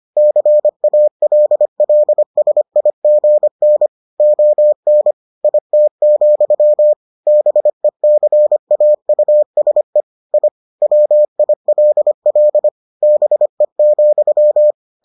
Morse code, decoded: CALLSIGN ON IT, BECAUSE I WILL BE,

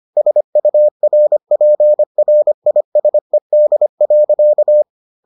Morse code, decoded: SURPRISED.

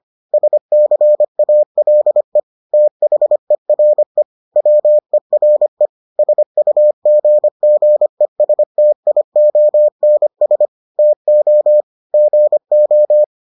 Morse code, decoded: SCALE THERE WERE SUGGESTIONS TO GO

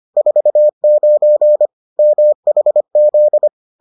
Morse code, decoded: 49 MHZ

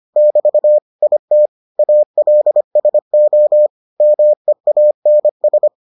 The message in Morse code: <BT> IT ALSO MEANS

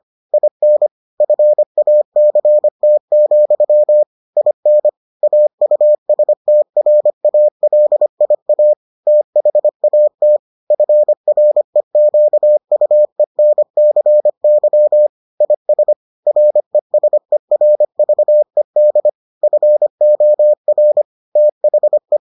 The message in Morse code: IN FACT, IN AUSTRALIA THAT FREQUENCY IS RESERVED FOR THE

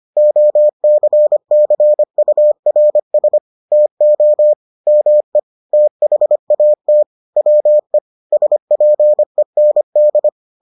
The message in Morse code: OCCURS TO ME THAT WE SPEND

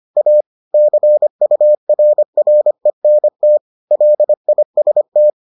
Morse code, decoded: A CURRENT LIST

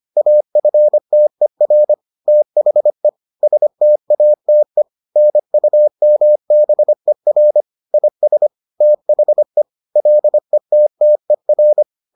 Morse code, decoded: AFTER THE STATE NUMBER IS THE LETTER

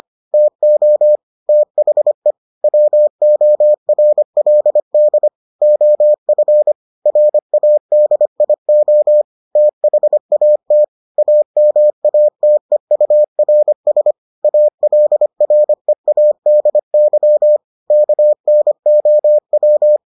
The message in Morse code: TO THE WORLD OF RADIO THAT AMATEURS ALREADY KNOW

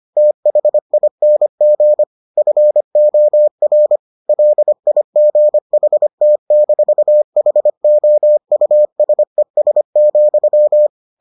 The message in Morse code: THING FOR LIGHT-HOUSES,

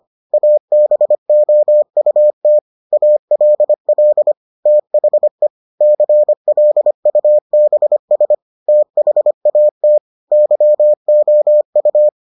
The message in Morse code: ABOUT ALL THE CLUBS THAT YOU

